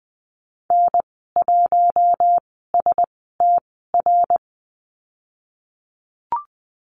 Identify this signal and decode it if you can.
Morse code, heard as a keyed tone, about 20 wpm, spaced as for 15 wpm, 700 Hz. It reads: N1STR
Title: 055 Lesson 10g - Letter r - Call Signs 20wpm